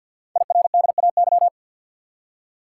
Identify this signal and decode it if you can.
Morse code, heard as a keyed tone, about 50 wpm, 690 Hz. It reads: IRDAY
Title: Code Groups - 5 Letters - 50wpm